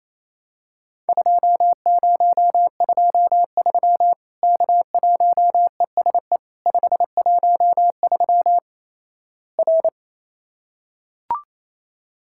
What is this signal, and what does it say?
Morse code, heard as a keyed tone, about 28 wpm, 700 Hz. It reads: 2023 K1ESE 513 R
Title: Worked All Europe DX Contest - QTC Exchange - 28wpm